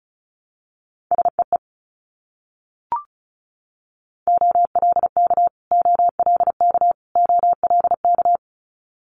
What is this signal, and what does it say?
Morse code, heard as a keyed tone, about 36 wpm, 700 Hz. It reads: SEE OLK OLK OLK
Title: Multi-letter Phonemes Repeated 3x 35wpm